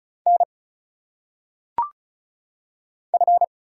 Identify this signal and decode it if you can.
Morse code, heard as a keyed tone, about 36 wpm, 700 Hz. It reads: N F
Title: Single Letters - Phonetic Pronunciation 35wpm